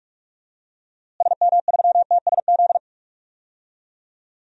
Morse code, heard as a keyed tone, about 46 wpm, 700 Hz. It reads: SM3TSZ
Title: DXCC-100 Call Signs - 6 Characters - Repeated 2x - 45wpm